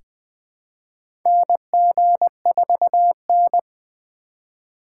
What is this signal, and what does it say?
Morse code, heard as a keyed tone, about 20 wpm, 700 Hz. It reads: NG4N
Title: USA Call Signs - 4 Characters - 8x Character Spacing - 20wpm